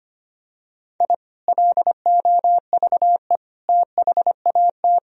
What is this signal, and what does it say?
Morse code, heard as a keyed tone, about 25 wpm, 700 Hz. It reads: I LOVE THAT
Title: Sentences from Top 1000 Words - Review Part 1 - 25wpm